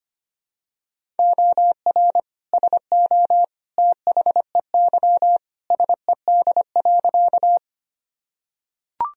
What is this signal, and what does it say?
Morse code, heard as a keyed tone, about 25 wpm, 700 Hz. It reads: OR SO THEY SED.
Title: Sentences from Top 100 Words and Abbreviations 25wpm